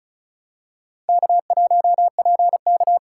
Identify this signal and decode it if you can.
Morse code, heard as a keyed tone, about 34 wpm, 700 Hz. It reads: K1PK
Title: USA Call Signs - 4 Characters - 4x Character Spacing - 35wpm